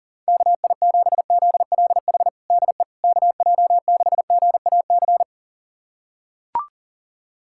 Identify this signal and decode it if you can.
Morse code, heard as a keyed tone, about 40 wpm, 700 Hz. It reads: KI7ZLH DE KJ6GAC
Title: USA Callsign DE Callsign - 6 Characters - 40wpm